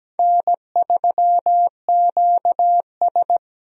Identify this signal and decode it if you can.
Morse code, heard as a keyed tone, about 17 wpm, 700 Hz. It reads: N3QS
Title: USA Call Signs - 4 Characters - Repeated 3x - 17wpm